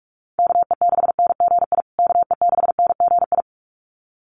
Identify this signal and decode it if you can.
Morse code, heard as a keyed tone, about 44 wpm, 700 Hz. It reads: KE6NGI KE6NGI
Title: USA Call Signs Repeated 2x 45wpm